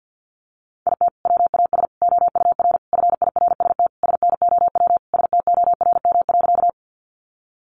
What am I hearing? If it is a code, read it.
Morse code, heard as a keyed tone, about 50 wpm, 700 Hz. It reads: IT WAS OUR FIRST SNOW STORM.
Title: Sentences from Top 1500 Words 50wpm